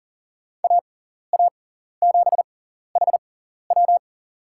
Morse code, heard as a keyed tone, about 40 wpm, spaced as for 14 wpm, 700 Hz. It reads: AA7HW
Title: USA Call Signs - 5 Characters - 6x Character Spacing - 40wpm